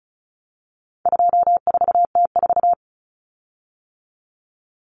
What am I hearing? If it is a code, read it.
Morse code, heard as a keyed tone, about 34 wpm, 700 Hz. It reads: 24T4
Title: Common Cut Numbers - 4-Digit 35wpm